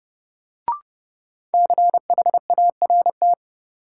Morse code, heard as a keyed tone, about 30 wpm, 700 Hz. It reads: CHART
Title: Sentences from Top 1300 Words - New Word Drill - 30wpm